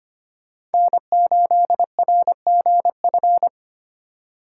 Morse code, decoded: N8RGF

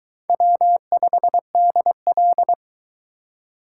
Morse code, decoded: W5DL

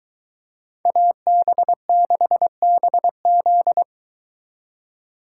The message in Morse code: AB6BZ

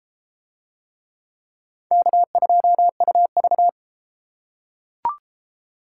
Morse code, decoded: K2UV

